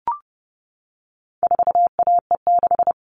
Morse code, heard as a keyed tone, about 28 wpm, 700 Hz. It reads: 4AE6
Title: All Cut Numbers - 4-Digit 30wpm